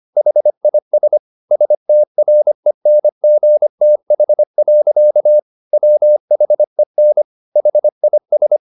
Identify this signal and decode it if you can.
Morse code, heard as a keyed tone, about 25 wpm, 600 Hz. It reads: HIS STRENGTH. WHEN HIS